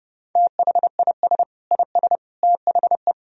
Morse code, heard as a keyed tone, about 30 wpm, 700 Hz. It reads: THIS IS THE
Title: Sentences from Top 1000 Words - Review Part 2 - 30wpm